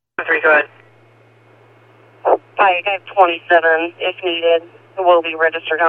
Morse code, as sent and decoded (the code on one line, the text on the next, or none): none